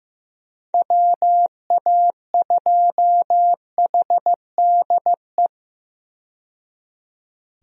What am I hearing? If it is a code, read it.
Morse code, heard as a keyed tone, about 15 wpm, 700 Hz. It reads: WA2HDE
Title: USA Call Signs - 6 Characters - 15wpm